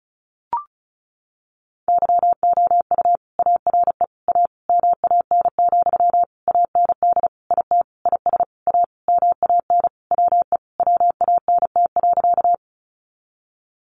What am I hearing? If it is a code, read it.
Morse code, heard as a keyed tone, about 34 wpm, 700 Hz. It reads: YOU ARE A MAN, AND IT IS A MAN WE WANT.
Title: Sentences from Top 200 Words 35wpm